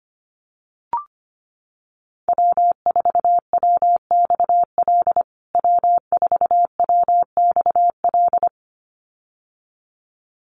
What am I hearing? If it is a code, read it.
Morse code, heard as a keyed tone, about 25 wpm, 700 Hz. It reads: W4WXL W4WXL
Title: USA Call Signs Repeated 2x 25wpm